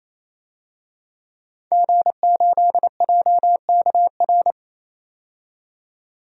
Morse code, heard as a keyed tone, about 27 wpm, 700 Hz. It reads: G8JKR
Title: DXCC-100 Call Signs - 5 Characters - 28wpm